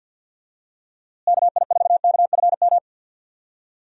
Morse code, heard as a keyed tone, about 48 wpm, 700 Hz. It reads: KI4KFM
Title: USA Call Signs - 6 Characters - Repeated 3x - 50wpm